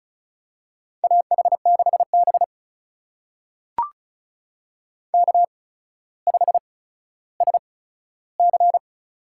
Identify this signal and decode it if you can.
Morse code, heard as a keyed tone, about 36 wpm, spaced as for 15 wpm, 700 Hz. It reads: AH6B K5SC
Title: USA Call Signs - 4 Characters - 8x Character Spacing - 35wpm